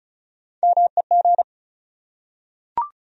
Morse code, heard as a keyed tone, about 36 wpm, 700 Hz. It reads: MEG
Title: USA Common Names - Max 4 Letters - 35wpm